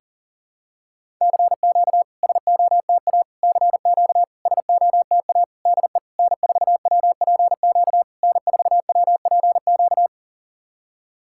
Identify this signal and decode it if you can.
Morse code, heard as a keyed tone, about 37 wpm, 690 Hz. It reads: CQ SOTA CQ SOTA DE N4WPQ N4WPQ
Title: SOTA QSO Practice 40wpm